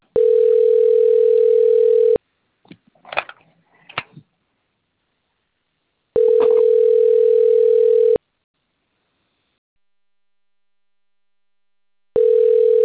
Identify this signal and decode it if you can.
Ringback tone (North America), 440 + 480 Hz.